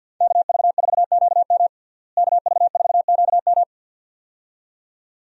Morse code, heard as a keyed tone, about 52 wpm, 700 Hz. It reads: KV4QM KV4QM
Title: USA Call Signs - 5 Characters - Repeated 2x - 50wpm